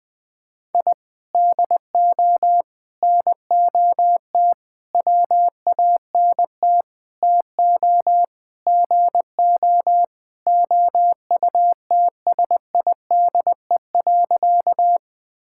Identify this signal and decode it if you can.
Morse code, heard as a keyed tone, about 21 wpm, 700 Hz. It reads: I DO NOT WANT TO GO OUTSIDE.
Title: Sentences with 5-Word Phrases - 20wpm